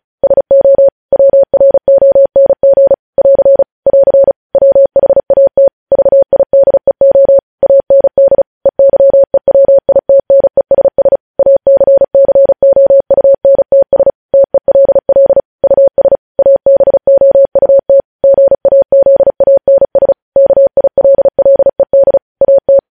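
Morse code, heard as a keyed tone, about 36 wpm, 550 Hz. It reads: SO WRONG <AR> <AR> WHAT VIDEO AND EYEWITNESS ACCOUNTS TELL US ABOUT GAZANS KILLED AT
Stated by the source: Morse code news headlines sent at 35.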